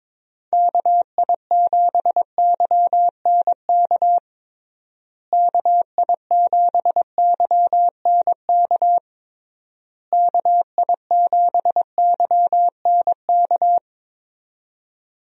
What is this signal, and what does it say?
Morse code, heard as a keyed tone, about 21 wpm, 700 Hz. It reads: KI7YNK KI7YNK KI7YNK
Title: USA Call Signs - 6 Characters - Repeated 3x - 22wpm